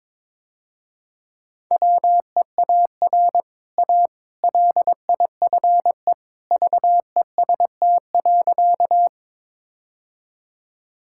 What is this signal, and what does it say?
Morse code, heard as a keyed tone, about 22 wpm, 700 Hz. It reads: WEAR A LIFE VEST.
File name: practice / 22wpm / Easy Sentences with Hints - 22wpm